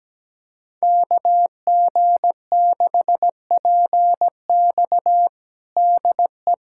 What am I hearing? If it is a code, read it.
Morse code, heard as a keyed tone, about 17 wpm, 700 Hz. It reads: KG6PX DE
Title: USA Callsign DE Callsign - 5 Characters - 17wpm